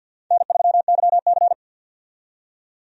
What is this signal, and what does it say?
Morse code, heard as a keyed tone, about 50 wpm, 700 Hz. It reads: N3YC
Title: USA Call Signs - 4 Characters - 50wpm